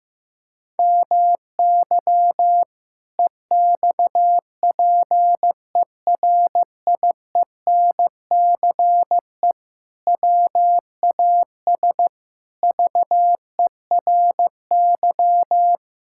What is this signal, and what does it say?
Morse code, heard as a keyed tone, about 16 wpm, 700 Hz. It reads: MY EXPERIENCE WAS VERY
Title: Sentences from Top 1000 Words - Review Part 6 - 15wpm